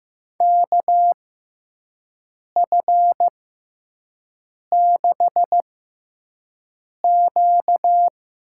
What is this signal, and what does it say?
Morse code, heard as a keyed tone, about 15 wpm, spaced as for 5 wpm, 700 Hz. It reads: KF6Q
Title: USA Call Signs - 6 Characters - 6x Character Spacing - 15wpm